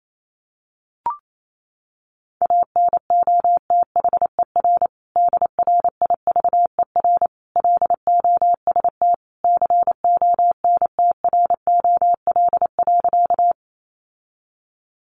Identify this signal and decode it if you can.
Morse code, heard as a keyed tone, about 28 wpm, 700 Hz. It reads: ANOTHER DRIVER LOST CONTROL.
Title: Sentences from Top 1400 Words 28wpm